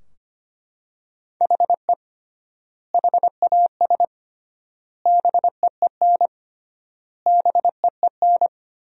Morse code, heard as a keyed tone, about 25 wpm, 700 Hz. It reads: HE HAS BEEN BEEN
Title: 3x Word-Spacing - Sentences from Top 300 Words - 25wpm